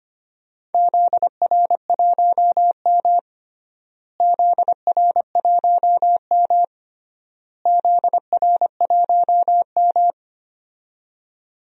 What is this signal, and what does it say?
Morse code, heard as a keyed tone, about 25 wpm, 700 Hz. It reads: ZR1M ZR1M ZR1M
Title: DXCC-100 Call Signs - 4 Characters - Repeated 3x - 25wpm